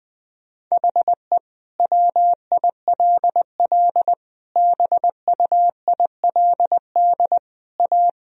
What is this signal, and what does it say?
Morse code, heard as a keyed tone, about 21 wpm, 700 Hz. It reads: HE WILL BUILD A
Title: Sentences from Top 1000 Words - Review Part 1 - 20wpm